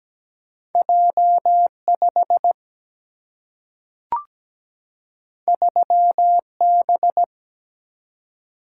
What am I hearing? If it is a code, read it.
Morse code, heard as a keyed tone, about 17 wpm, 700 Hz. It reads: J5 3B
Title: Single-Letter Single-Number 17wpm